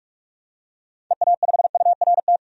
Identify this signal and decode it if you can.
Morse code, heard as a keyed tone, about 46 wpm, 700 Hz. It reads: EA5URT
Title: DXCC-100 Call Signs - 6 Characters - 45wpm